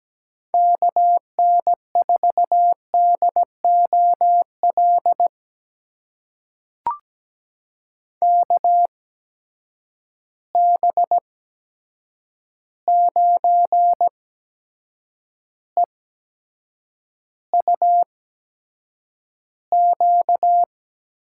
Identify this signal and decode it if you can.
Morse code, heard as a keyed tone, about 17 wpm, spaced as for 7 wpm, 700 Hz. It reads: KN4DOL KB9EUQ
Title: USA Call Signs - 6 Characters - 8x Character Spacing - 17wpm